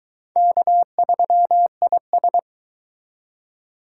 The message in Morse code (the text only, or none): K3IS